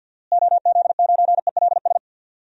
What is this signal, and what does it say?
Morse code, heard as a keyed tone, about 46 wpm, 700 Hz. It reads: OZ9ELS
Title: DXCC-100 Call Signs - 6 Characters - Repeated 2x - 50wpm